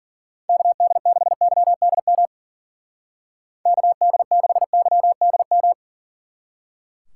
Morse code, heard as a keyed tone, about 49 wpm, 700 Hz. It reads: KD6YDM KD6YDM
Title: Speed Racing - USA Call Signs 40wpm